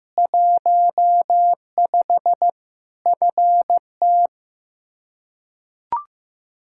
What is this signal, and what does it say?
Morse code, heard as a keyed tone, about 15 wpm, 700 Hz. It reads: 15 FT
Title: Practice QSO - Exchange 2 - Info - 15wpm